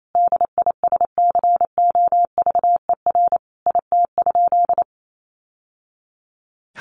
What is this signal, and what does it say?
Morse code, heard as a keyed tone, about 28 wpm, 700 Hz. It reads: DISCOVER IT?
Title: Sentences from Top 1100 Words 28wpm